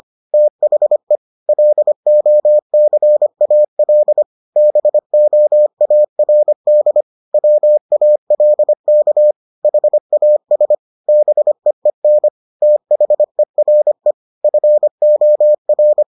Morse code, heard as a keyed tone, about 25 wpm, 600 Hz. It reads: THE LOCAL BOARD WALK HAS BEEN THERE FOR